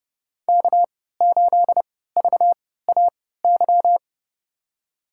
Morse code, heard as a keyed tone, about 30 wpm, spaced as for 17 wpm, 700 Hz. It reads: K8VAY
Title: USA Call Signs - 5 Characters - 3x Character Spacing - 30wpm